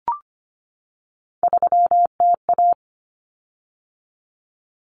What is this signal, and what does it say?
Morse code, heard as a keyed tone, about 25 wpm, 700 Hz. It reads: 3TA
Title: Common Cut Numbers - 3-Digit 25wpm